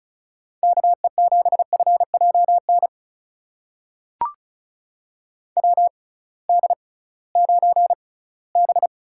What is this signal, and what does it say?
Morse code, heard as a keyed tone, about 34 wpm, spaced as for 20 wpm, 700 Hz. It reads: KE7FJN WD9B
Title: USA Call Signs - 6 Characters - 6x Character Spacing - 35wpm